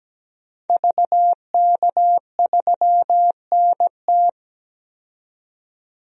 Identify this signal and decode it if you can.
Morse code, heard as a keyed tone, about 17 wpm, 700 Hz. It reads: VK3NT
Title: DXCC-100 Call Signs - 5 Characters - 17wpm